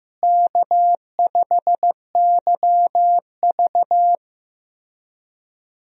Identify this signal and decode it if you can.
Morse code, heard as a keyed tone, about 15 wpm, 700 Hz. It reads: K5YV
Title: USA Call Signs - 4 Characters - Repeated 2x - 15wpm